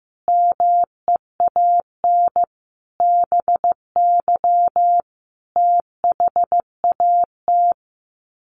Morse code, heard as a keyed tone, about 15 wpm, 700 Hz. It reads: MEAN BY THAT
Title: Sets of 5 Words Encore2 15wpm